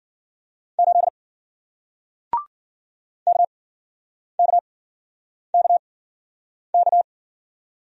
Morse code, heard as a keyed tone, about 53 wpm, spaced as for 9 wpm, 700 Hz. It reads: 8 KKKK